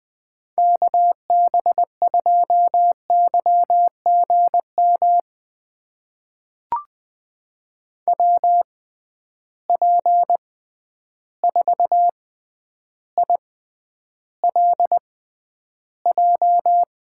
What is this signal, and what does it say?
Morse code, heard as a keyed tone, about 20 wpm, spaced as for 10 wpm, 700 Hz. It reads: KB2YGM WP4ILJ
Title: USA Call Signs - 6 Characters - 6x Character Spacing - 20wpm